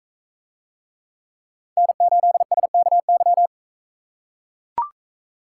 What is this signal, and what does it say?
Morse code, heard as a keyed tone, about 42 wpm, 700 Hz. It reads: N8SKY